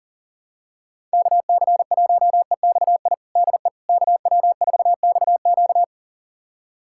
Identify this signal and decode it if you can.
Morse code, heard as a keyed tone, about 39 wpm, 700 Hz. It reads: KC1EXI DE KW4XQ
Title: USA Callsign DE Callsign 40wpm